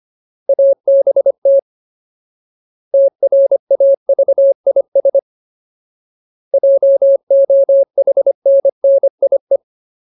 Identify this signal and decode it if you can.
Morse code, heard as a keyed tone, about 25 wpm, 550 Hz. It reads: ABT TRAVIS JOHNNIE
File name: Ultimate Word List / 25WPM Ultimate